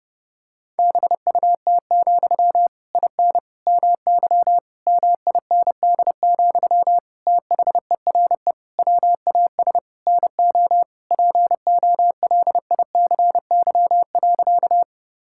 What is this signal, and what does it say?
Morse code, heard as a keyed tone, about 30 wpm, 700 Hz. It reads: BUT, IN MY MIND, THERE WAS NO POLICY.
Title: Sentences from Top 1000 Words - Review Part 5 - 30wpm